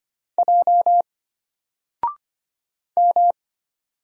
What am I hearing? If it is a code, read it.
Morse code, heard as a keyed tone, about 25 wpm, 700 Hz. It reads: J M
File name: Single-Letter-Number-25wpm